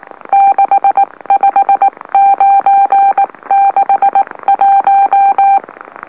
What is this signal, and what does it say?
Morse code, heard as a keyed tone, about 19 wpm, spaced as for 17 wpm, 790 Hz. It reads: 65961
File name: M04- -2345utc-LOLO